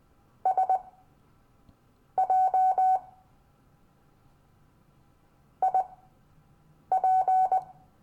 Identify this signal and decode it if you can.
Morse code, heard as a keyed tone, about 20 wpm, spaced as for 6 wpm, 750 Hz. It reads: SJ IP
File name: cw / Lesson15 five forwardslash